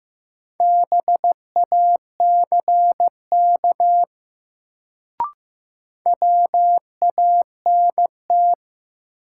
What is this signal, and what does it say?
Morse code, heard as a keyed tone, about 15 wpm, 700 Hz. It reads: BACK WANT
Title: Sentences from Top 200 Words - New Word Drill - 15wpm